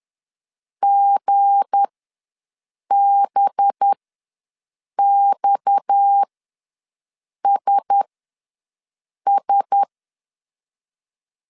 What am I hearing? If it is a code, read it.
Morse code, heard as a keyed tone, about 10 wpm, spaced as for 6 wpm, 800 Hz. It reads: GBXSS
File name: LEZIONI / 8 YJ45 / 8 YJ ws 10wpm 4043